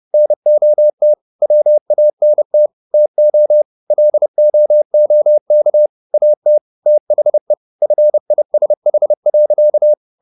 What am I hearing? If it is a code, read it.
Morse code, heard as a keyed tone, about 30 wpm, 600 Hz. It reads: NOT WANT TO LOOK AT THE FISH.